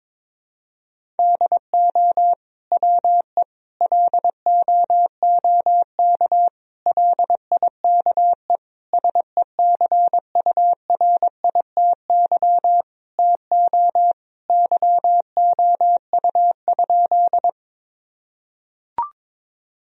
Morse code, heard as a keyed tone, about 22 wpm, 700 Hz. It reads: DO WE LOOK LIKE SECURITY TO YOU?
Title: Sentences from Top 1000 Words - Review Part 4 - 22wpm